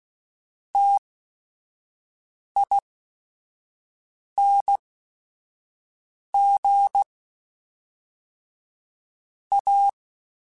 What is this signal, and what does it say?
Morse code, heard as a keyed tone, about 16 wpm, spaced as for 5 wpm, 800 Hz. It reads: TING A
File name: QSOs / QSOs-mp3s / WA7QOX 1@5